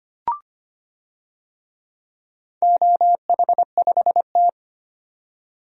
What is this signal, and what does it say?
Morse code, heard as a keyed tone, about 25 wpm, 690 Hz. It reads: OH5T
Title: DXCC-100 Call Signs - 4 Characters - 25wpm